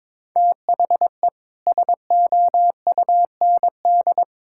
Morse code, heard as a keyed tone, about 22 wpm, 700 Hz. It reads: THE SOUND
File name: practice / 22wpm / Sentences with 3-Word Phrases - 22wpm